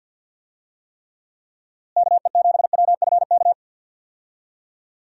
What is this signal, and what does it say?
Morse code, heard as a keyed tone, about 49 wpm, 700 Hz. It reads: KE7WFK
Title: USA Call Signs - 6 Characters - Repeated 2x - 50wpm